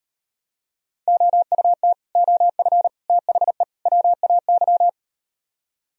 Morse code, heard as a keyed tone, about 38 wpm, 700 Hz. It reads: OUT OF THE WAY